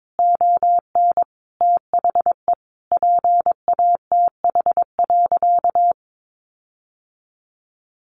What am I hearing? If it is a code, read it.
Morse code, heard as a keyed tone, about 22 wpm, 700 Hz. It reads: ON THE PATH.
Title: Sentences from Top 1400 Words 22wpm